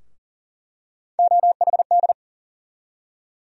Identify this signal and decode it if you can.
Morse code, heard as a keyed tone, about 38 wpm, 700 Hz. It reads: OHD
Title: USA Call Sign Postfixes - 3 Letters 40wpm